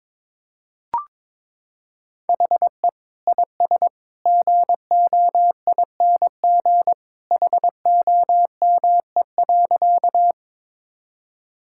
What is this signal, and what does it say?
Morse code, heard as a keyed tone, about 23 wpm, 700 Hz. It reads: HE IS GOING HOME.
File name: practice / 22wpm / Sentences with 2-Word Phrases - 22wpm